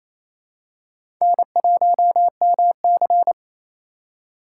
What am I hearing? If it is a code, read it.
Morse code, heard as a keyed tone, about 28 wpm, 700 Hz. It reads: N1MC
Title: USA Call Signs - 4 Characters - 8x Character Spacing - 28wpm